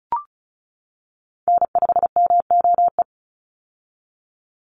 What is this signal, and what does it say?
Morse code, heard as a keyed tone, about 34 wpm, 700 Hz. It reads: N5MOE
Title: USA Call Signs 35wpm